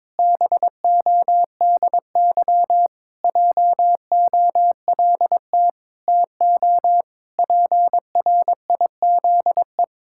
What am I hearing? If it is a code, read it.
Morse code, heard as a keyed tone, about 22 wpm, 700 Hz. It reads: BODY JOLT TO PRIZE
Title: Pangram Challenge v2 - 22wpm